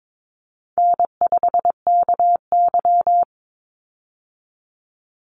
Morse code, heard as a keyed tone, about 22 wpm, 700 Hz. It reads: N5KY
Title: Easy Call Signs Repeated 3x - 22wpm